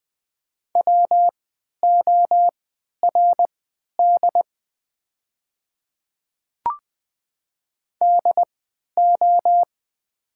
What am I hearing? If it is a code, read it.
Morse code, heard as a keyed tone, about 20 wpm, spaced as for 11 wpm, 700 Hz. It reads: WORD DO
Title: 3x Character Spacing - Top 100 Words - 20wpm